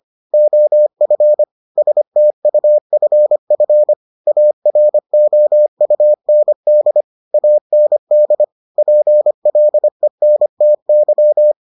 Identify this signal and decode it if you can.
Morse code, heard as a keyed tone, about 25 wpm, 600 Hz. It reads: OF STUFF AROUND AND PLENTY